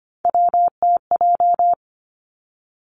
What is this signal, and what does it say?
Morse code, heard as a keyed tone, about 25 wpm, 700 Hz. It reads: WTJ
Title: Three Challenging Letter-Number Combinations 25wpm